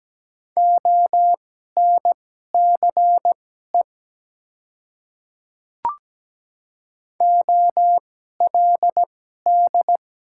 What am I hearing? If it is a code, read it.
Morse code, heard as a keyed tone, about 17 wpm, spaced as for 12 wpm, 700 Hz. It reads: ONCE OLD
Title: 2x Character Spacing - Top 300 Words - 17wpm